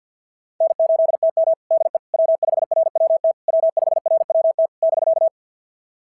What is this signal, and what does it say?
Morse code, heard as a keyed tone, about 45 wpm, 690 Hz. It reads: N8TM DE W5RWT W5RWT <BK>
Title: SOTA QSO Practice 50wpm